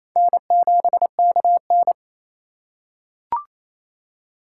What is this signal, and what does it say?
Morse code, heard as a keyed tone, about 27 wpm, 700 Hz. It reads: N7KN
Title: USA Call Signs - 4 Characters - 2x Character Spacing - 28wpm